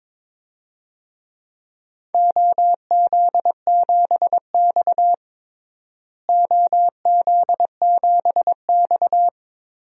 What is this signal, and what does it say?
Morse code, heard as a keyed tone, about 21 wpm, 700 Hz. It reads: OZ7X OZ7X
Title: DXCC-100 Call Signs - 4 Characters - Repeated 3x - 22wpm